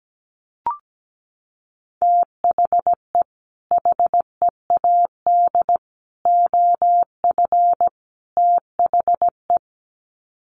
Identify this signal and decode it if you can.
Morse code, heard as a keyed tone, about 17 wpm, 700 Hz. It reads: THE HEAD OF THE
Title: Sets of 4 Words 17wpm